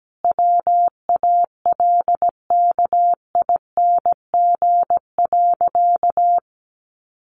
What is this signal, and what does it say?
Morse code, heard as a keyed tone, about 17 wpm, 700 Hz. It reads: WALKING.
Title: Additional Easy Phrases 17wpm